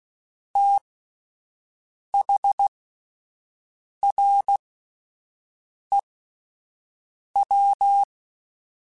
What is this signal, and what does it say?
Morse code, heard as a keyed tone, about 16 wpm, spaced as for 5 wpm, 800 Hz. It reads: THREW